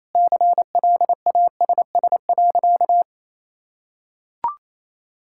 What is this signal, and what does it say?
Morse code, heard as a keyed tone, about 28 wpm, 700 Hz. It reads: CLASS.
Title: Sentences from Top 1000 Words - Review Part 6 - 28wpm